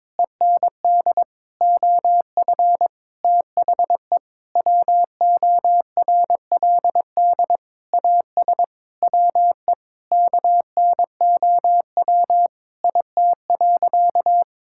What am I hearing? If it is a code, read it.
Morse code, heard as a keyed tone, about 22 wpm, 700 Hz. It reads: END OF THE WORLD AS WE KNOW IT.
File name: Sentences with 9-Word Phrases - 22wpm